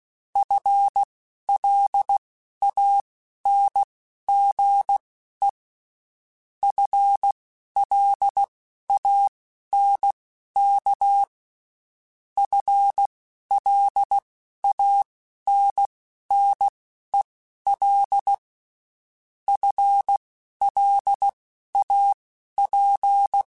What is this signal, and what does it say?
Morse code, heard as a keyed tone, about 16 wpm, spaced as for 11 wpm, 800 Hz. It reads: FLANGE FLANK FLANNEL FLAP
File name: Spaced / FL Runs sp mp3s / FL Runs sp @15